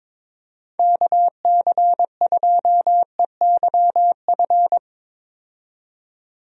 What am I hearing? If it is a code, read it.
Morse code, heard as a keyed tone, about 22 wpm, 700 Hz. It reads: KC2EYF
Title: USA Call Signs - 6 Characters - 22wpm